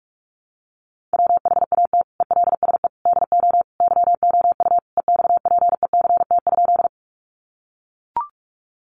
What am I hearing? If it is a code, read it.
Morse code, heard as a keyed tone, about 44 wpm, 700 Hz. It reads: WHAT ELSE DO YOU EXPECT?
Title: Sentences from Top 1100 Words 45wpm